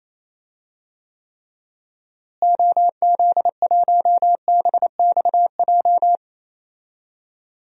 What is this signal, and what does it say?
Morse code, heard as a keyed tone, about 28 wpm, 700 Hz. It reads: OZ1BXJ
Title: DXCC-100 Call Signs - 6 Characters - Repeated 3x - 28wpm